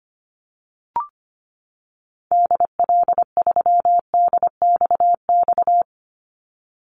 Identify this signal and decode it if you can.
Morse code, heard as a keyed tone, about 25 wpm, 700 Hz. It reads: DL3DXX
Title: Call Signs - 25wpm